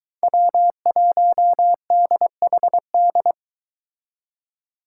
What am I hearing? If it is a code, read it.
Morse code, heard as a keyed tone, about 23 wpm, 700 Hz. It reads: W1DHD